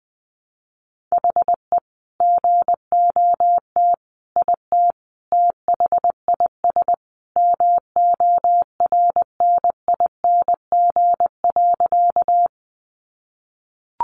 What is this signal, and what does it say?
Morse code, heard as a keyed tone, about 19 wpm, 700 Hz. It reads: HE GOT IT THIS MORNING.
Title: Sentences from Top 400 Words 20wpm